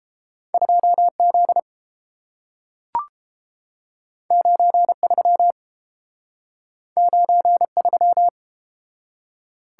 Morse code, heard as a keyed tone, about 30 wpm, 700 Hz. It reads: 2Z 93 93